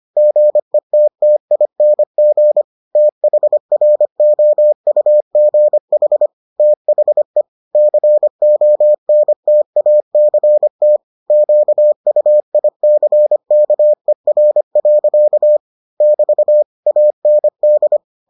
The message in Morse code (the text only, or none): GETTING THROUGH THE CONTACT QUICKER. <BT> AND